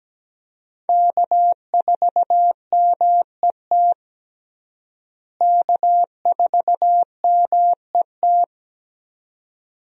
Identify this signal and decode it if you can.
Morse code, heard as a keyed tone, about 17 wpm, 700 Hz. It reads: K4MET K4MET
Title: USA Call Signs - 5 Characters - Repeated 2x - 17wpm